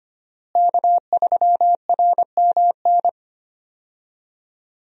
Morse code, heard as a keyed tone, about 25 wpm, 700 Hz. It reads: K3RMN
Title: USA Call Signs - 5 Characters - Repeated 3x - 25wpm